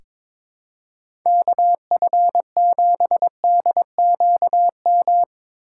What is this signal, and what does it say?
Morse code, heard as a keyed tone, about 23 wpm, 700 Hz. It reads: KF7DQM
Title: USA Call Signs - 6 Characters - 6x Character Spacing - 22wpm